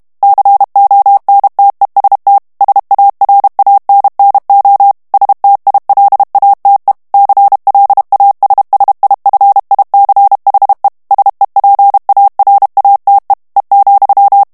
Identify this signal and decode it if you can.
Morse code, heard as a keyed tone, about 32 wpm, 800 Hz. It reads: CONTEST SARANNO STILATE CLASSIFICHE SEPARATE E,
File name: ITALIEN texte clair 2 vitesse 30 mots minute